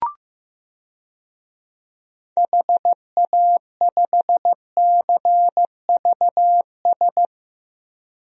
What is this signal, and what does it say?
Morse code, heard as a keyed tone, about 15 wpm, 700 Hz. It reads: HA5CVS
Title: DXCC-100 Call Signs - 6 Characters - Repeated 2x - 15wpm